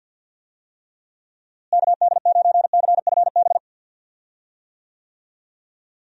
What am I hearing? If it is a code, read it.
Morse code, heard as a keyed tone, about 50 wpm, 700 Hz. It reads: KD9CFB
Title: USA Call Signs - 6 Characters - Repeated 2x - 50wpm